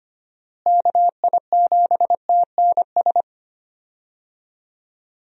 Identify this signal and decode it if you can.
Morse code, heard as a keyed tone, about 25 wpm, 700 Hz. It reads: KI7TNS
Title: USA Call Signs - 6 Characters - Repeated 3x - 25wpm